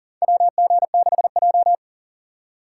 Morse code, heard as a keyed tone, about 40 wpm, 700 Hz. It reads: WG6J
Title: USA Call Signs - 4x Character Spacing - 40wpm